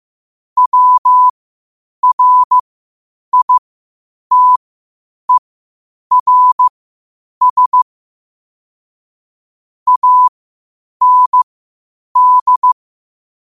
Morse code, heard as a keyed tone, about 15 wpm, spaced as for 8 wpm, 1000 Hz. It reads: WRITERS AND
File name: qotdfiles / 8wpm